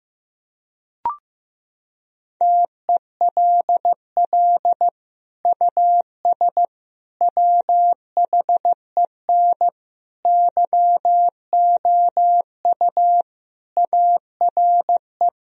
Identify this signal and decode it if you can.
Morse code, heard as a keyed tone, about 16 wpm, 700 Hz. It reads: TELL US WHEN YOU ARE
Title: Sentences from Top 1000 Words - Review Part 2 - 15wpm